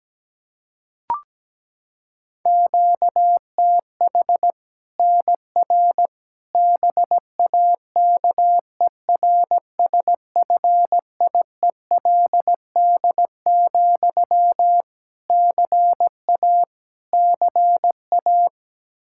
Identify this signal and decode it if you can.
Morse code, heard as a keyed tone, about 17 wpm, 700 Hz. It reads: QTH NR BAKERSFIELD, CA CA
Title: Practice QSO - Exchange 1 - Beginning - 17wpm